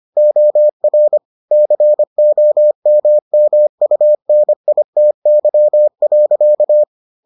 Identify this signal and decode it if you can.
Morse code, heard as a keyed tone, about 25 wpm, 600 Hz. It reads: OR COMMUNITY.